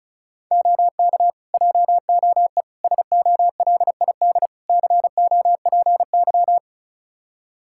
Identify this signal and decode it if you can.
Morse code, heard as a keyed tone, about 36 wpm, 700 Hz. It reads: OK JOE SOLID COPY
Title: K1USN SST QSO Practice 35wpm